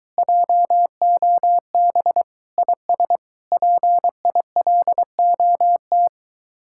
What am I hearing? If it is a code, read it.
Morse code, heard as a keyed tone, about 23 wpm, 700 Hz. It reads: JOB IS PILOT